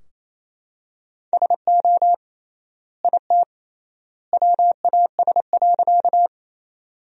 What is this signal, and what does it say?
Morse code, heard as a keyed tone, about 27 wpm, 700 Hz. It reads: SO IT WAS.
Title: 3x Word-Spacing - Sentences from Top 100 Words - 28wpm